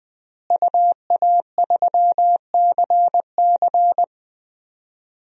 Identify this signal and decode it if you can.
Morse code, heard as a keyed tone, about 20 wpm, 700 Hz. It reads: UA3CC
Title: DXCC-100 Call Signs - 5 Characters - Repeated 2x - 20wpm